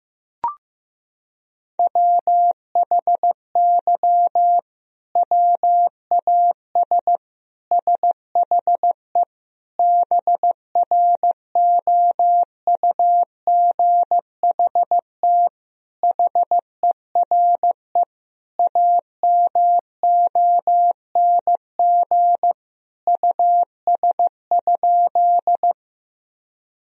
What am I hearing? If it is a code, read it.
Morse code, heard as a keyed tone, about 15 wpm, 700 Hz. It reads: WHY WAS SHE BROUGHT HERE AMONG US?
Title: Sentences from Top 1000 Words - Review Part 3 - 15wpm